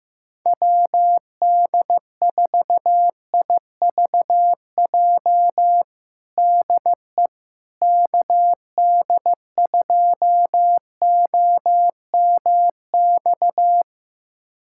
Morse code, heard as a keyed tone, about 15 wpm, 700 Hz. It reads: WD4IVJ DE KD2OMX
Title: USA Callsign DE Callsign 15wpm